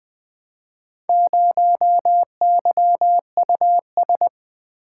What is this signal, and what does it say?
Morse code, heard as a keyed tone, about 20 wpm, 700 Hz. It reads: 0YUS
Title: Four Letter-Number Combinations - 20wpm